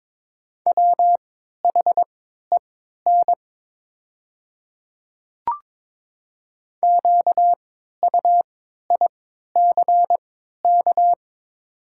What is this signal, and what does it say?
Morse code, heard as a keyed tone, about 22 wpm, spaced as for 12 wpm, 700 Hz. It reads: WHEN QUICK
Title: 3x Character Spacing - Top 500 Words - 22wpm